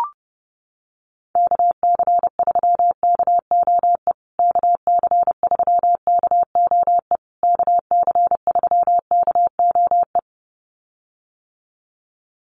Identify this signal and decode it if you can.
Morse code, heard as a keyed tone, about 30 wpm, 700 Hz. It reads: KC3KOE KC3KOE KC3KOE
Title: USA Call Signs Repeated 3x 30wpm